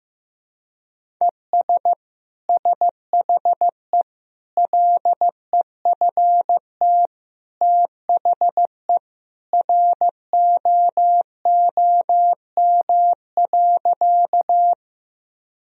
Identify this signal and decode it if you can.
Morse code, heard as a keyed tone, about 16 wpm, 700 Hz. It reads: ES SHE LEFT THE ROOM.
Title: Sentences from Top 300 Words and Abbreviations 15wpm